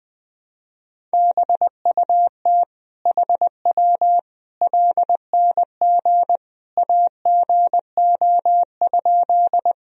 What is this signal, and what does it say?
Morse code, heard as a keyed tone, about 20 wpm, 700 Hz. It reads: BUT HW LNG AGO?
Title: Sentences from Top 500 Words and Abbreviations 20wpm